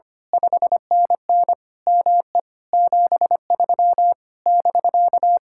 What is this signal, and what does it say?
Morse code, heard as a keyed tone, about 24 wpm, 700 Hz. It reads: 5NN ME 73 <BK>
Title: Speed-Racing Common QSO Phrases - 25wpm